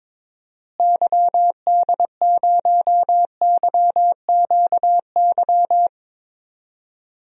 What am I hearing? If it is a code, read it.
Morse code, heard as a keyed tone, about 21 wpm, 700 Hz. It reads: YD0YQY
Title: DXCC-100 Call Signs - 6 Characters - Repeated 2x - 22wpm